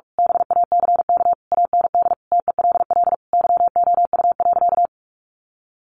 Morse code, heard as a keyed tone, about 44 wpm, 700 Hz. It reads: BACK AND TELL YOU.
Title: Sentences from Top 1100 Words 45wpm